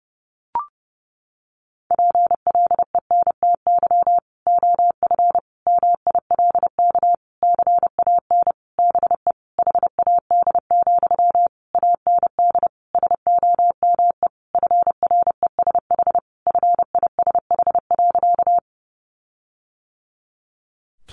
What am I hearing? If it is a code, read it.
Morse code, heard as a keyed tone, about 30 wpm, 700 Hz. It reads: PLENTY OF MILK CAN BE HAD, AND SOME FRESH FISH.
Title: Sentences from Top 1500 Words 30wpm